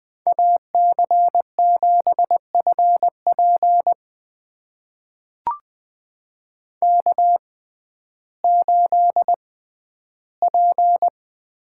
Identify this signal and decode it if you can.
Morse code, heard as a keyed tone, about 20 wpm, spaced as for 12 wpm, 700 Hz. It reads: AC7FP K8P
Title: USA Call Signs - 5 Characters - 6x Character Spacing - 20wpm